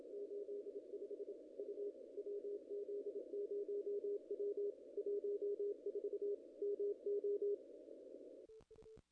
Morse code, heard as a keyed tone, about 32 wpm, 410 Hz. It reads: 85 U WZ0W 14 MO
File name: contests / ss-cw-k5tr-op-n5rz -0000